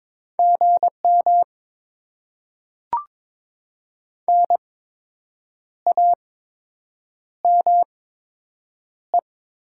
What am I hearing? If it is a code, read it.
Morse code, heard as a keyed tone, about 22 wpm, spaced as for 7 wpm, 700 Hz. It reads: GM NAME
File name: Top 20 QSO Elements - 8x Character Spacing - 22wpm